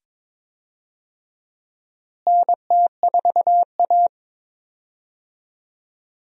Morse code, heard as a keyed tone, about 22 wpm, 700 Hz. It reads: NT4A
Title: USA Call Signs - 4 Characters - Repeated 2x - 22wpm